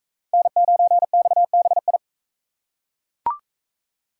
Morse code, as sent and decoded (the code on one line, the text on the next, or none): -. ----. -..- -... ..
N9XBI